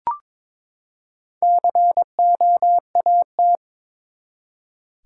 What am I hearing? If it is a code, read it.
Morse code, heard as a keyed tone, about 23 wpm, 700 Hz. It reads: COAT
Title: Four Letter Words - 22wpm